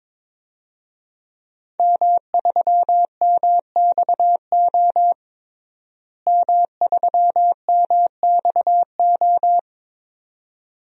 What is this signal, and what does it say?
Morse code, heard as a keyed tone, about 22 wpm, 700 Hz. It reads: M3MXO M3MXO
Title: DXCC-100 Call Signs - 5 Characters - Repeated 2x - 22wpm